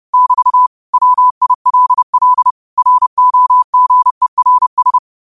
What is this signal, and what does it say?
Morse code, heard as a keyed tone, about 30 wpm, 1000 Hz. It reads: <BT> WILL ROGERS
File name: qotdfiles / 30wpm